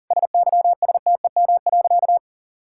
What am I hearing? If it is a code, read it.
Morse code, heard as a keyed tone, about 40 wpm, 700 Hz. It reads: SYSTEM.
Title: Sentences from Top 1000 Words - Review Part 5 - 40wpm